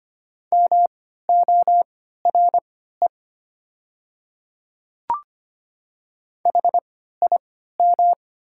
Morse code, heard as a keyed tone, about 25 wpm, spaced as for 14 wpm, 700 Hz. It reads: MORE HIM
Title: 3x Character Spacing - Top 100 Words - 25wpm